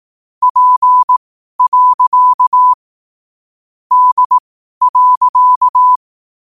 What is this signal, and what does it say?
Morse code, heard as a keyed tone, about 18 wpm, spaced as for 12 wpm, 1000 Hz. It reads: P. D.
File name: qotdfiles / 12wpm